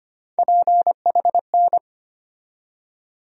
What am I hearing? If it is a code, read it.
Morse code, heard as a keyed tone, about 25 wpm, 700 Hz. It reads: PHN
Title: USA Call Sign Postfixes - 3 Letters 25wpm